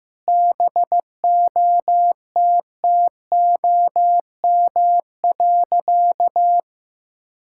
Morse code, heard as a keyed tone, about 16 wpm, 700 Hz. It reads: BOTTOM.
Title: Sentences from Top 1000 Words - Review Part 6 - 15wpm